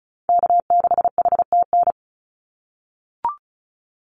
Morse code, heard as a keyed tone, about 34 wpm, 700 Hz. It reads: K6HTN
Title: Call Signs - 35wpm